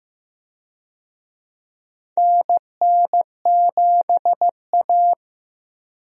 Morse code, heard as a keyed tone, about 16 wpm, 700 Hz. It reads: NN7A
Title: USA Call Signs - 4 Characters - Repeated 2x - 15wpm